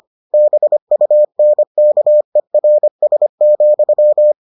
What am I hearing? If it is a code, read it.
Morse code, heard as a keyed tone, about 25 wpm, 600 Hz. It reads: BUNKERS,